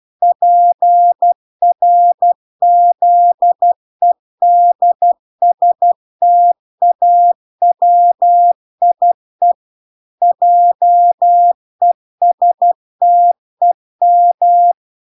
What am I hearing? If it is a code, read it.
Morse code, heard as a keyed tone, about 12 wpm, 700 Hz. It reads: PRZEDSTAWIE JESTEM